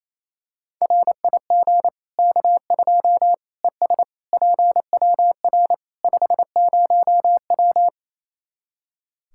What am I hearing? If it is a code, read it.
Morse code, heard as a keyed tone, about 28 wpm, 700 Hz. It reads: RIG K2 ES PWR 50W
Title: Practice QSO - Exchange 2 - Info - 28wpm